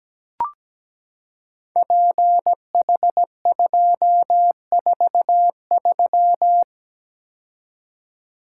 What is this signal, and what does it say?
Morse code, heard as a keyed tone, about 17 wpm, 700 Hz. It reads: PH243
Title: Code Groups - 5 Letter:Numbers - 17wpm